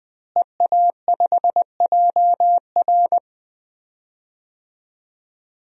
Morse code, heard as a keyed tone, about 20 wpm, 700 Hz. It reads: EA5JR
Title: DXCC-100 Call Signs - 5 Characters - Repeated 2x - 20wpm